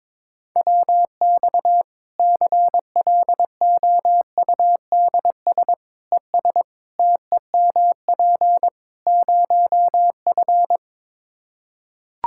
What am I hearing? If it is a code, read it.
Morse code, heard as a keyed tone, about 22 wpm, 700 Hz. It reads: WX CLOUDS ES TEMP 0F
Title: Practice QSO - Exchange 2 - Info - 22wpm